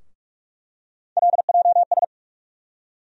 Morse code, heard as a keyed tone, about 43 wpm, 700 Hz. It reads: LJS
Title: USA Call Sign Postfixes - 3 Letters 45wpm